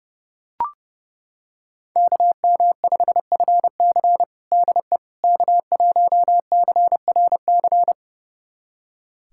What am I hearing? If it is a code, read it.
Morse code, heard as a keyed tone, about 28 wpm, 700 Hz. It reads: KM5FC DE K1CRC
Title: USA Callsign DE Callsign - 5 Characters - 30wpm